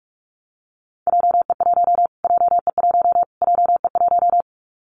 Morse code, heard as a keyed tone, about 44 wpm, 700 Hz. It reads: JE1 JE1 JE1
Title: Callsign Prefixes for Top 25 DXCC - Repeated 3x - 45wpm